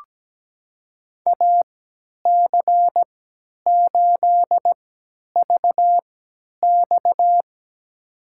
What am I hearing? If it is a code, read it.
Morse code, heard as a keyed tone, about 17 wpm, spaced as for 10 wpm, 700 Hz. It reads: AC8VX
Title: USA Call Signs - 5 Characters - 3x Character Spacing - 17wpm